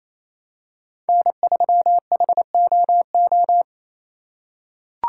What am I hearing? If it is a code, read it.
Morse code, heard as a keyed tone, about 28 wpm, 700 Hz. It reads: N3HOO
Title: USA Call Signs - 5 Characters - 4x Character Spacing - 28wpm